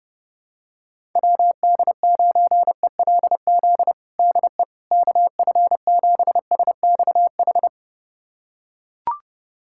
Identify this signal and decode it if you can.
Morse code, heard as a keyed tone, about 30 wpm, 700 Hz. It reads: WD9ELZ DE KF7SXH
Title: USA Callsign DE Callsign - 6 Characters - 30wpm